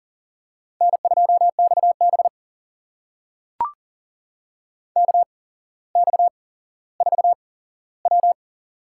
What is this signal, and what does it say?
Morse code, heard as a keyed tone, about 38 wpm, spaced as for 17 wpm, 700 Hz. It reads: N2XB KX4W
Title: USA Call Signs - 4 Characters - 8x Character Spacing - 40wpm